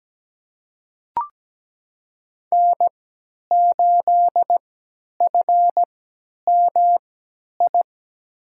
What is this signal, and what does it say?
Morse code, heard as a keyed tone, about 17 wpm, spaced as for 10 wpm, 700 Hz. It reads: N8FMI
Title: USA Call Signs - 5 Characters - 3x Character Spacing - 17wpm